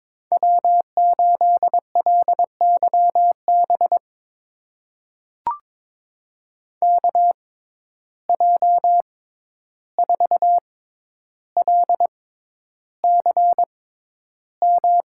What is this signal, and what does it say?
Morse code, heard as a keyed tone, about 22 wpm, spaced as for 11 wpm, 700 Hz. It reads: W8LYB KJ4LCM
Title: USA Call Signs - 6x Character Spacing - 22wpm